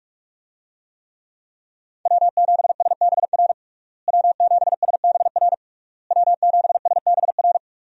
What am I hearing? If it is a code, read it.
Morse code, heard as a keyed tone, about 46 wpm, 700 Hz. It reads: W7SBR W7SBR W7SBR
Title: USA Call Signs - 5 Characters - Repeated 3x - 45wpm